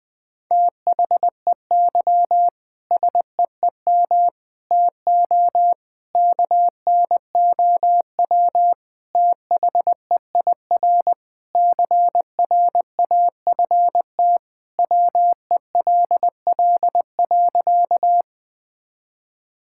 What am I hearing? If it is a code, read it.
Morse code, heard as a keyed tone, about 20 wpm, 700 Hz. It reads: THEY SEEM TO KNOW THEIR CRAFT WELL.
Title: Sentences from Top 1000 Words - Review Part 6 - 20wpm